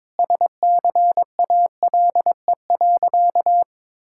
Morse code, heard as a keyed tone, about 22 wpm, 700 Hz. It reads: SCALE.
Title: Sentences from Top 1000 Words - Review Part 5 - 22wpm